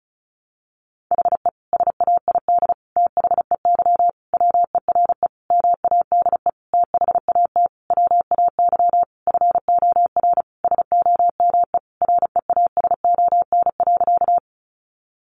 Morse code, heard as a keyed tone, about 34 wpm, 700 Hz. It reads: HE SAID THEY WERE MADE THAT WAY FOR SOME REASON.
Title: Sentences from Top 700 Words 35wpm